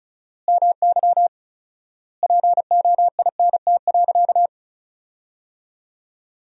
Morse code, heard as a keyed tone, about 34 wpm, 700 Hz. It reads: MY POINT.
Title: 4x Word-Spacing - Sentences from Top 200 Words - 35wpm